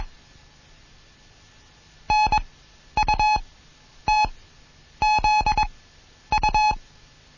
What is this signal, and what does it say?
Morse code, heard as a keyed tone, about 21 wpm, spaced as for 10 wpm, 850 Hz. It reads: NUTZU
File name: Morsen / Kurs A 20